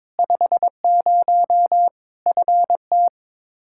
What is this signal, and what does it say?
Morse code, heard as a keyed tone, about 22 wpm, 700 Hz. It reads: 50 FT
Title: Practice QSO - Exchange 2 - Info - 22wpm